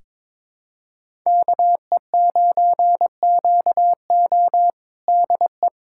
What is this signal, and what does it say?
Morse code, heard as a keyed tone, about 21 wpm, 700 Hz. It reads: KE9QO DE
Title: USA Callsign DE Callsign - 5 Characters - 22wpm